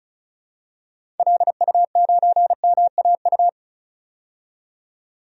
Morse code, heard as a keyed tone, about 36 wpm, 700 Hz. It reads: LU9MAU
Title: DXCC-100 Call Signs - 6 Characters - 35wpm